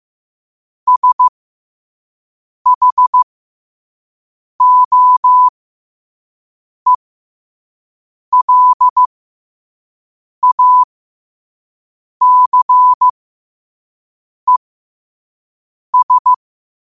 Morse code, heard as a keyed tone, about 15 wpm, spaced as for 5 wpm, 1000 Hz. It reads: SHOELACES
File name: qotdfiles / 5wpm